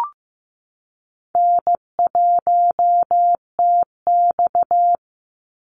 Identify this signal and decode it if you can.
Morse code, heard as a keyed tone, about 15 wpm, 700 Hz. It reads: N1TX
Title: Easy Call Signs - 15wpm